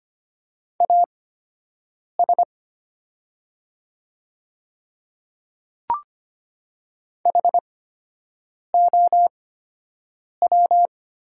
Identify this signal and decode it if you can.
Morse code, heard as a keyed tone, about 25 wpm, spaced as for 7 wpm, 700 Hz. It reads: AS HOW
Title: 8x Character Spacing - Top 200 Words - 25wpm